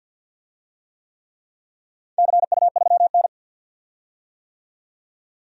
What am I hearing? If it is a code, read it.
Morse code, heard as a keyed tone, about 51 wpm, 700 Hz. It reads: CU3N
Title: DXCC-100 Call Signs - 4 Characters - 50wpm